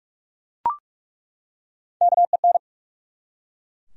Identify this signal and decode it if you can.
Morse code, heard as a keyed tone, about 48 wpm, 700 Hz. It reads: KEN
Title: USA Common Names - Max 3 Letters - 45wpm